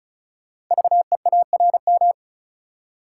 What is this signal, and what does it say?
Morse code, heard as a keyed tone, about 36 wpm, 690 Hz. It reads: VEARM
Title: Code Groups - 5 Letters - 35wpm